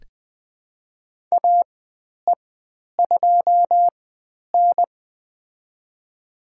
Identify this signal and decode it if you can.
Morse code, heard as a keyed tone, about 20 wpm, spaced as for 10 wpm, 700 Hz. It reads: AE2N
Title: 095 Lesson 16g - Letter c - Call Signs 20wpm